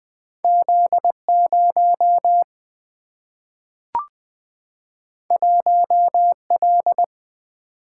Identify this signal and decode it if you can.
Morse code, heard as a keyed tone, about 20 wpm, 700 Hz. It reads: Z 0 1 L
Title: Single-Letter Single-Number 20wpm